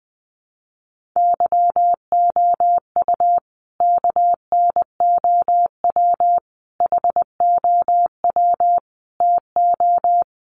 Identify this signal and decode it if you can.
Morse code, heard as a keyed tone, about 20 wpm, 700 Hz. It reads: YOU KNOW HOW TO
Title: Sentences from Top 1300 Words 20wpm